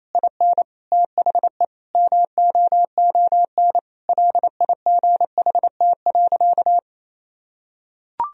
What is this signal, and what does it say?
Morse code, heard as a keyed tone, about 28 wpm, 700 Hz. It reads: IN THE MOON LIGHT.
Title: Sentences from Top 1000 Words - Review Part 4 - 28wpm